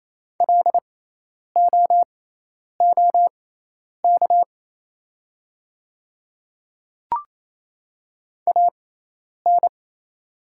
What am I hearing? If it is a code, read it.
Morse code, heard as a keyed tone, about 27 wpm, spaced as for 10 wpm, 700 Hz. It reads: LOOK AN
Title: 6x Character Spacing - Top 100 Words - 28wpm